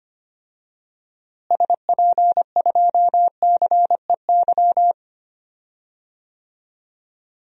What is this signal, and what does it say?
Morse code, heard as a keyed tone, about 25 wpm, 700 Hz. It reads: SP2CEY
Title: DXCC-100 Call Signs - 6 Characters - Repeated 3x - 25wpm